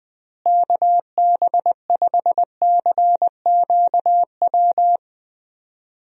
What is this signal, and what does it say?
Morse code, heard as a keyed tone, about 20 wpm, 700 Hz. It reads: KB5CQW
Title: USA Call Signs - 6 Characters - Repeated 2x - 20wpm